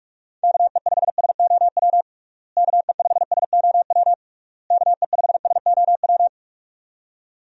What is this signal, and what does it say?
Morse code, heard as a keyed tone, about 46 wpm, 700 Hz. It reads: KE5SOW KE5SOW KE5SOW
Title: USA Call Signs - 6 Characters - Repeated 3x - 45wpm